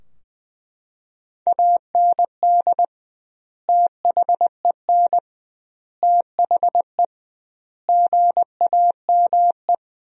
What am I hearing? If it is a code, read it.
Morse code, heard as a keyed tone, about 20 wpm, 700 Hz. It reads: AND THEN THE GAME
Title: 2x Word-Spacing - Sentences from Top 500 Words - 20wpm